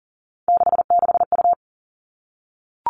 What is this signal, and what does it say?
Morse code, heard as a keyed tone, about 41 wpm, 700 Hz. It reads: 66U
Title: Three Challenging Letter-Number Combinations 40wpm